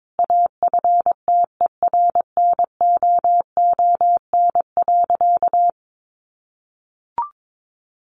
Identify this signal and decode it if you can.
Morse code, heard as a keyed tone, about 22 wpm, 700 Hz. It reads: AFTERNOON.
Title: Sentences from Top 1500 Words 22wpm